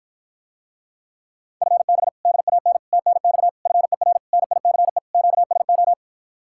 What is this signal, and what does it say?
Morse code, heard as a keyed tone, about 54 wpm, 700 Hz. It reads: FB DAN TNX FER NICE QSO